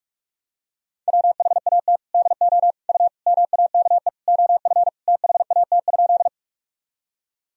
Morse code, heard as a keyed tone, about 42 wpm, 700 Hz. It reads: WHAT DO U MAKE OF THAT?
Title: Sentences from Top 100 Words and Abbreviations 45wpm